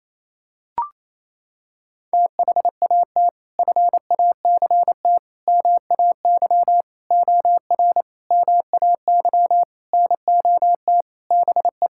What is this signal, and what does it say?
Morse code, heard as a keyed tone, about 27 wpm, 700 Hz. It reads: THAT FACT MAY OR MAY NOT BE
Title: Sentences from Top 1000 Words - Review Part 6 - 28wpm